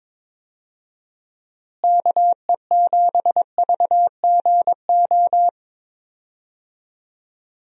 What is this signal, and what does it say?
Morse code, heard as a keyed tone, about 22 wpm, 700 Hz. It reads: KE7VGO
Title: USA Call Signs - 6 Characters - Repeated 3x - 22wpm